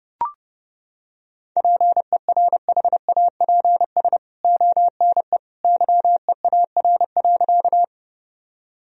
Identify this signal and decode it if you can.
Morse code, heard as a keyed tone, about 32 wpm, 700 Hz. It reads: PERHAPS ONE YEAR.
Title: Sentences from Top 1000 Words - Review Part 1 - 30wpm